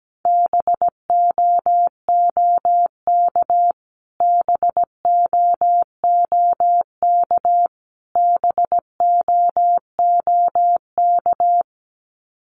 Morse code, heard as a keyed tone, about 17 wpm, 700 Hz. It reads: BOOK BOOK BOOK
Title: Top 300 Words Repeated 17wpm